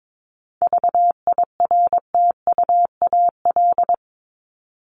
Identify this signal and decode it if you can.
Morse code, heard as a keyed tone, about 22 wpm, 700 Hz. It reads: VIRTUAL
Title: LY Suffix - Top 100 Words with Base Words 22wpm